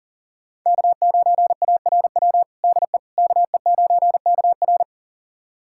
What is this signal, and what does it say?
Morse code, heard as a keyed tone, about 40 wpm, 700 Hz. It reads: K9ARW DE KE9KR
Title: USA Callsign DE Callsign - 5 Characters - 40wpm